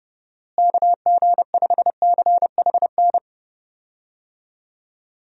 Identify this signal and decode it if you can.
Morse code, heard as a keyed tone, about 30 wpm, 700 Hz. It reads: KG5CHN
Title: USA Call Signs - 6 Characters - Repeated 3x - 30wpm